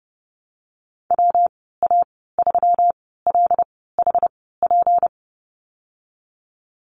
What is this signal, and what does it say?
Morse code, heard as a keyed tone, about 32 wpm, spaced as for 17 wpm, 700 Hz. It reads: WA3LHP
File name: Lesson 37g - Letter x - Call Signs 30wpm